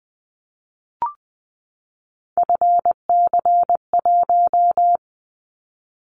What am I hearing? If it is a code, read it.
Morse code, heard as a keyed tone, about 20 wpm, 700 Hz. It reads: FC1
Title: Callsign Prefixes for Top 50 DXCC - 20wpm